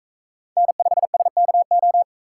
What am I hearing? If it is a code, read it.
Morse code, heard as a keyed tone, about 42 wpm, 700 Hz. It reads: N5SKO